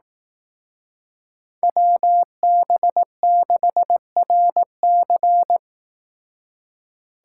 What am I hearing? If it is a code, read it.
Morse code, heard as a keyed tone, about 18 wpm, 700 Hz. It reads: WB6RC
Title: Speed Racing - USA Call Signs 15wpm